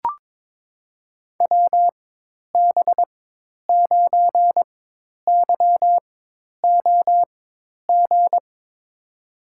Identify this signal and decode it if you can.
Morse code, heard as a keyed tone, about 22 wpm, spaced as for 10 wpm, 700 Hz. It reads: WB9YOG
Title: USA Call Signs - 6 Characters - 4x Character Spacing - 22wpm